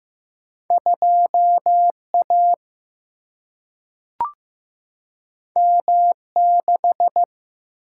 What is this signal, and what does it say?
Morse code, heard as a keyed tone, about 15 wpm, 700 Hz. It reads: 2 A M 6
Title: Single-Letter Single-Number 15wpm